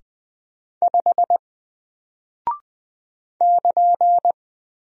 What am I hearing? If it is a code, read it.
Morse code, heard as a keyed tone, about 20 wpm, 700 Hz. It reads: 5 <KN>
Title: Single Character (Letter:Number:Punctuation:Prosign) Alt 20wpm